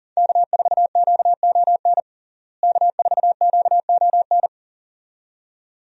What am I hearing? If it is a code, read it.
Morse code, heard as a keyed tone, about 40 wpm, 700 Hz. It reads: K4QON K4QON
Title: USA Call Signs - 5 Characters - Repeated 2x - 40wpm